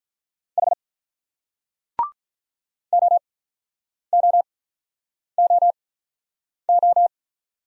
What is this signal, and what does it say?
Morse code, heard as a keyed tone, about 44 wpm, spaced as for 9 wpm, 700 Hz. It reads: H OOOO